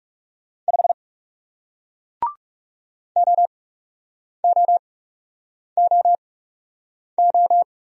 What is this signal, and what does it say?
Morse code, heard as a keyed tone, about 42 wpm, spaced as for 8 wpm, 700 Hz. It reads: 5 OOOO